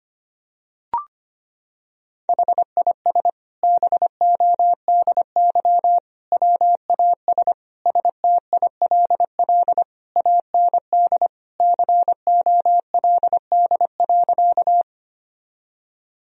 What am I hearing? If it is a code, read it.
Morse code, heard as a keyed tone, about 25 wpm, 700 Hz. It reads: HIS BODY WAS STILL AND COLD.
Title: Sentences from Top 1000 Words - Review Part 2 - 25wpm